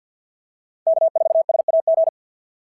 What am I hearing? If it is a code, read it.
Morse code, heard as a keyed tone, about 50 wpm, 690 Hz. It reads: K4SAG
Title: SOTA QSO Practice 50wpm